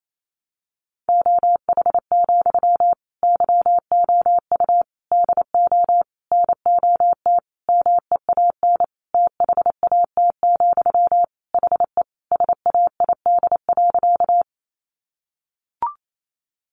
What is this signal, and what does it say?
Morse code, heard as a keyed tone, about 28 wpm, 700 Hz. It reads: OH, YOU DO NOT MEAN THAT, HE SAID.
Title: Sentences from Top 600 Words 28wpm